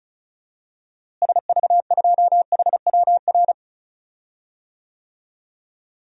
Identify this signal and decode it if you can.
Morse code, heard as a keyed tone, about 36 wpm, 700 Hz. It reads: SV2HWR
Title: DXCC-100 Call Signs - 6 Characters - Repeated 2x - 35wpm